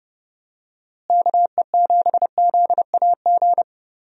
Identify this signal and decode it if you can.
Morse code, heard as a keyed tone, about 30 wpm, 700 Hz. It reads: KE7ZAG
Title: USA Call Signs - 6 Characters - Repeated 3x - 30wpm